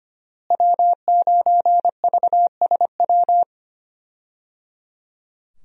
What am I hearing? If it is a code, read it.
Morse code, heard as a keyed tone, about 25 wpm, 700 Hz. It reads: W9VSW
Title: USA Call Signs - 5 Characters - Repeated 3x - 25wpm